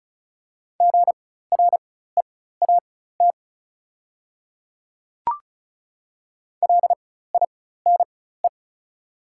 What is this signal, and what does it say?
Morse code, heard as a keyed tone, about 36 wpm, spaced as for 16 wpm, 700 Hz. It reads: GREAT LINE
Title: 4x Character Spacing - Top 200 Words - 35wpm